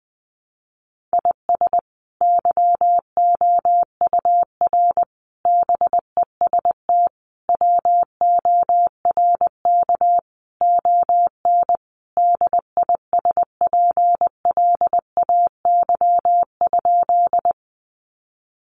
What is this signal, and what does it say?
Morse code, heard as a keyed tone, about 20 wpm, 700 Hz. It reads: IS YOUR BEST WORK ON DISPLAY?
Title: Sentences from Top 1200 Words 20wpm